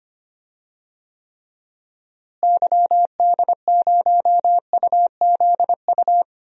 Morse code, heard as a keyed tone, about 25 wpm, 700 Hz. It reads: YD0UZU
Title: DXCC-100 Call Signs - 6 Characters - Repeated 3x - 25wpm